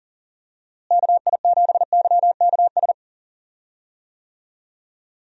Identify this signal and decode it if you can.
Morse code, heard as a keyed tone, about 37 wpm, 700 Hz. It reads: KI7YKS
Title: USA Call Signs - 6 Characters - Repeated 3x - 40wpm